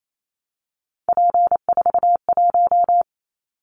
Morse code, heard as a keyed tone, about 28 wpm, 700 Hz. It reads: P41
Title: Callsign Prefixes for Top 100 DXCC - 28wpm